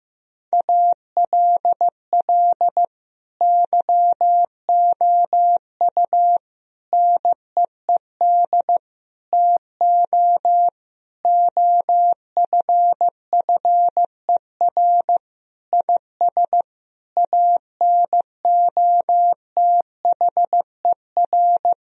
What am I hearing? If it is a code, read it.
Morse code, heard as a keyed tone, about 15 wpm, 700 Hz. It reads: ALL YOU NEED TO OFFER IS ANOTHER
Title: Sentences from Top 1000 Words - Review Part 4 - 15wpm